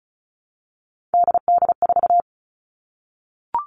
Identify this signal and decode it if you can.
Morse code, heard as a keyed tone, about 34 wpm, 700 Hz. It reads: DD4
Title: All Cut Numbers - 3-Digit 35wpm